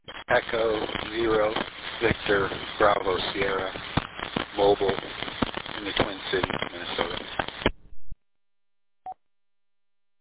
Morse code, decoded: none

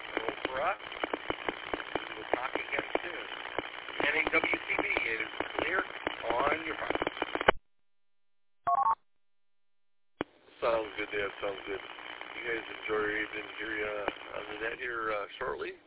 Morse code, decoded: none

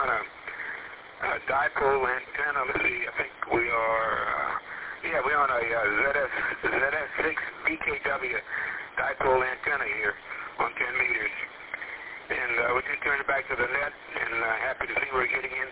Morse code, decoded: none

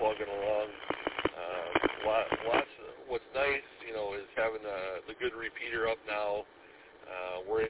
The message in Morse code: none

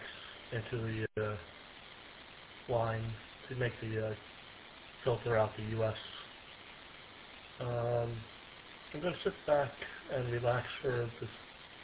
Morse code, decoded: none

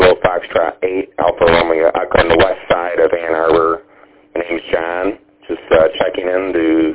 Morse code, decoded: none

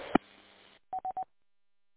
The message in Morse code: none

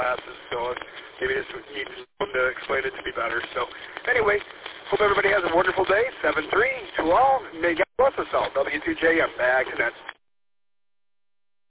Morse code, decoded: none

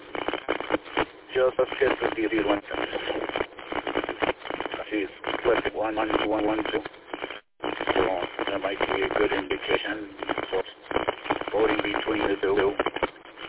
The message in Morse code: none

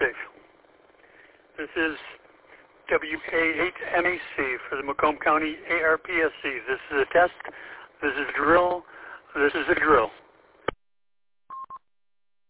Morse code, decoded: none

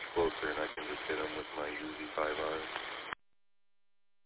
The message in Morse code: none

